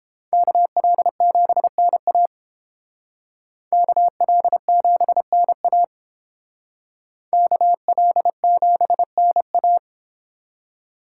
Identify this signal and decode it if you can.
Morse code, heard as a keyed tone, about 28 wpm, 700 Hz. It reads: KL7NA KL7NA KL7NA